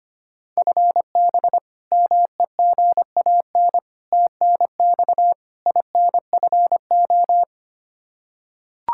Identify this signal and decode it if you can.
Morse code, heard as a keyed tone, about 25 wpm, 700 Hz. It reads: FB MEGAN TNX INFO
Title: Practice QSO - Exchange 3 - More Info - 25wpm